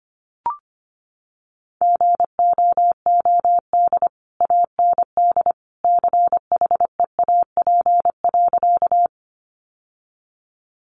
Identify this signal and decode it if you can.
Morse code, heard as a keyed tone, about 25 wpm, 700 Hz. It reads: GOOD AND CHEAP.
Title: Sentences from Top 1200 Words 25wpm